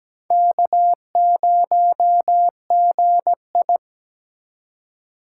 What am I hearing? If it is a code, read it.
Morse code, heard as a keyed tone, about 20 wpm, 700 Hz. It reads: K0GI
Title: Speed Racing - USA 1x2 Call Signs 17wpm